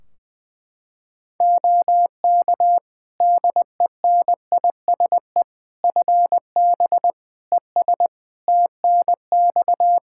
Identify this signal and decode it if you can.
Morse code, heard as a keyed tone, about 20 wpm, 700 Hz. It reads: OK DENISE FB ES TNX
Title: Practice QSO - Exchange 2 - Info - 20wpm